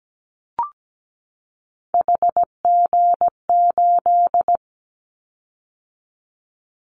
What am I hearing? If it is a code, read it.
Morse code, heard as a keyed tone, about 17 wpm, 700 Hz. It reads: HG8
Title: Callsign Prefixes for Top 25 DXCC - 17wpm